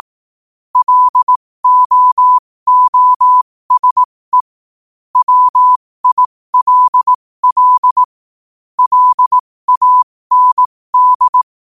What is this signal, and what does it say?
Morse code, heard as a keyed tone, about 18 wpm, spaced as for 15 wpm, 1000 Hz. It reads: LOOSE WILL LAND
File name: qotdfiles / 15wpm